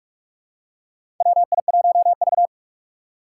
Morse code, heard as a keyed tone, about 44 wpm, 700 Hz. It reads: WI1V
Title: USA Call Signs - 4 Characters - 3x Character Spacing - 45wpm